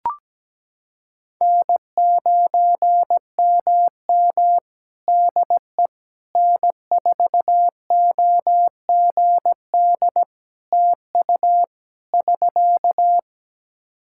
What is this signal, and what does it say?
Morse code, heard as a keyed tone, about 17 wpm, 700 Hz. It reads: N9MM DE N4OGD TU <SK>
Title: K1USN SST QSO Practice 17wpm